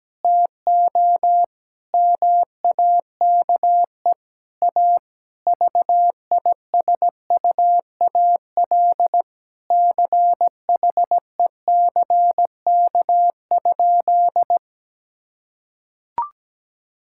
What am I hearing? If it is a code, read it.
Morse code, heard as a keyed tone, about 17 wpm, 700 Hz. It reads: TO MAKE A VISUAL CHECK?
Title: Sentences from Top 1000 Words - Review Part 6 - 17wpm